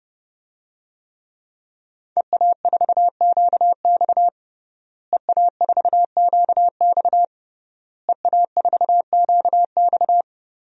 Morse code, heard as a keyed tone, about 32 wpm, 700 Hz. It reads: EA4QX EA4QX EA4QX
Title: DXCC-100 Call Signs - 5 Characters - Repeated 3x - 30wpm